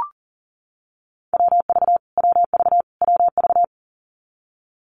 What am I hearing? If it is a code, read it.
Morse code, heard as a keyed tone, about 40 wpm, 700 Hz. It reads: WV WV WV
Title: US States Repeated 3x 40wpm